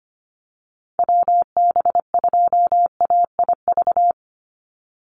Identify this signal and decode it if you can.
Morse code, heard as a keyed tone, about 25 wpm, 700 Hz. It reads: WB2AIV
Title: Call Signs Repeated 3x - 25wpm